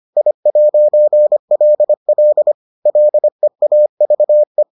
Morse code, heard as a keyed tone, about 25 wpm, 600 Hz. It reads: I'LL LEAVE